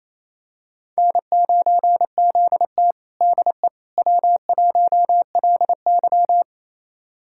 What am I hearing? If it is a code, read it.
Morse code, heard as a keyed tone, about 28 wpm, 700 Hz. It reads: N9ZT DE W1LY
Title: USA Callsign DE Callsign - 4 Characters - 28wpm